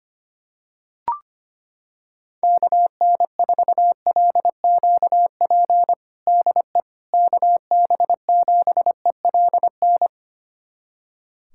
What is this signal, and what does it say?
Morse code, heard as a keyed tone, about 25 wpm, 700 Hz. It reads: KN4LQP DE KB7ELN
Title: USA Callsign DE Callsign 25wpm